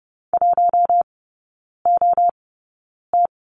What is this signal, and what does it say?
Morse code, heard as a keyed tone, about 30 wpm, spaced as for 9 wpm, 700 Hz. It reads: 1OT
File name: Lesson 15e - Number 5 - Three Characters 30wpm